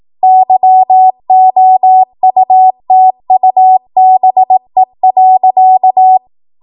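Morse code, heard as a keyed tone, about 18 wpm, 750 Hz. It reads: YOUTUBE.
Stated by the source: ARRL 18 wpm Morse code practice archive transmission.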